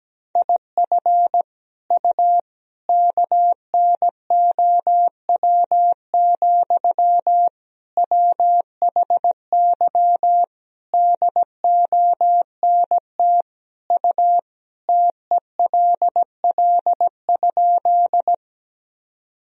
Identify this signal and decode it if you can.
Morse code, heard as a keyed tone, about 17 wpm, 700 Hz. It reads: IF U KNOW, WHY DONT U TELL?
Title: Sentences from Top 300 Words and Abbreviations 17wpm